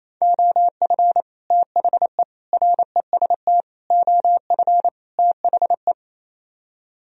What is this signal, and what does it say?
Morse code, heard as a keyed tone, about 37 wpm, 700 Hz. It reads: OF THE REST OF THE